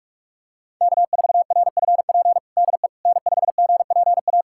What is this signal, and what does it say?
Morse code, heard as a keyed tone, about 46 wpm, 700 Hz. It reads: K4RFP DE N5GPA
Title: USA Callsign DE Callsign - 5 Characters - 45wpm